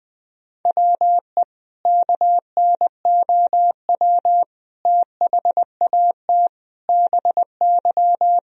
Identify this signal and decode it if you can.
Morse code, heard as a keyed tone, about 20 wpm, 700 Hz. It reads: WE KNOW THAT BY
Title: Sentences from Top 1000 Words - Review Part 3 - 20wpm